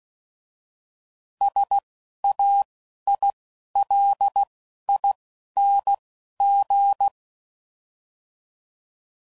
Morse code, heard as a keyed tone, about 16 wpm, 800 Hz. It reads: S A I L I N G
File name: Suffixes / Special INGs / Single INGs